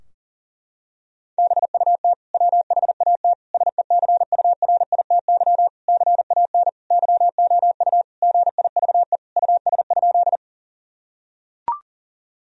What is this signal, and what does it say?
Morse code, heard as a keyed tone, about 37 wpm, 700 Hz. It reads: BUT WHAT SECURITY CAN YOU GIVE US?
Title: Sentences from Top 1000 Words - Review Part 4 - 40wpm